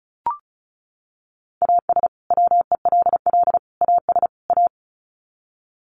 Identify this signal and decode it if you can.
Morse code, heard as a keyed tone, about 34 wpm, 700 Hz. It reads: AS WELL AS A
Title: Sets of 4 Words 35wpm